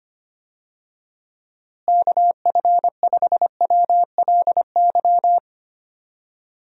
Morse code, heard as a keyed tone, about 25 wpm, 700 Hz. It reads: KF5WLY
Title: USA Call Signs - 6 Characters - Repeated 3x - 25wpm